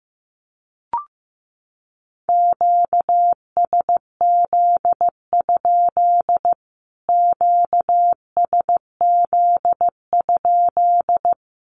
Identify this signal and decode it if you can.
Morse code, heard as a keyed tone, about 14 wpm, 700 Hz. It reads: QSZ? QSZ?
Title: Q-Codes Repeated 3x 15wpm